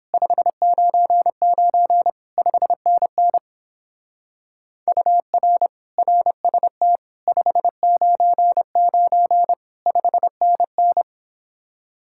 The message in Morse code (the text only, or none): 599 5NN UR RST 599 5NN